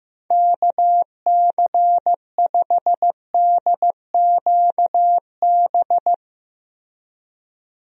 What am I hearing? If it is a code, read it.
Morse code, heard as a keyed tone, about 15 wpm, 700 Hz. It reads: KC5DQB
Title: USA Call Signs - 6 Characters - Repeated 3x - 15wpm